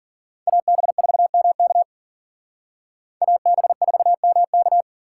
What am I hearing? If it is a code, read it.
Morse code, heard as a keyed tone, about 51 wpm, 700 Hz. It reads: AB4MK AB4MK